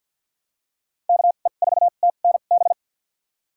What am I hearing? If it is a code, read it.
Morse code, heard as a keyed tone, about 50 wpm, spaced as for 36 wpm, 700 Hz. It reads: KE4TNB
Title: USA Call Signs - 2x Character Spacing - 50wpm